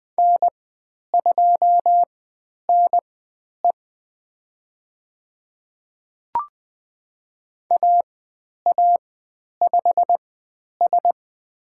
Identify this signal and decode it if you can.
Morse code, heard as a keyed tone, about 20 wpm, spaced as for 10 wpm, 700 Hz. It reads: N2NE AA5S
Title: 109 Lesson 18g - Letter m - Call Signs 20wpm